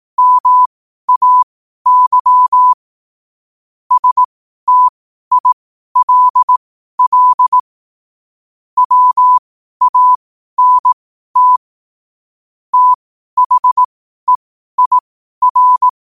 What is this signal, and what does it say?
Morse code, heard as a keyed tone, about 18 wpm, spaced as for 12 wpm, 1000 Hz. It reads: MAY STILL WANT THEIR
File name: qotdfiles / 12wpm